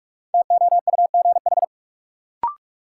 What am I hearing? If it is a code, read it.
Morse code, heard as a keyed tone, about 41 wpm, 700 Hz. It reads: TOUGH
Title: Sentences from Top 1200 Words - New Word Drill - 45wpm